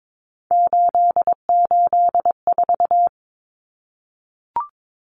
Morse code, heard as a keyed tone, about 22 wpm, 700 Hz. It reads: 884
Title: Common Cut Numbers - 3-Digit 22wpm